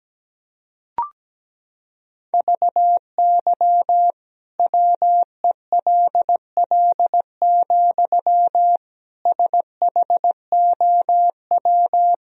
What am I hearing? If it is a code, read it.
Morse code, heard as a keyed tone, about 17 wpm, 700 Hz. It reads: VY WELL, SHOW
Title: Sentences from Top 200 Words and Abbreviations 17wpm